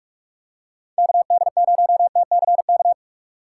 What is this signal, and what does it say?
Morse code, heard as a keyed tone, about 46 wpm, 700 Hz. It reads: KD0TCK
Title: USA Call Signs - 6 Characters - 3x Character Spacing - 45wpm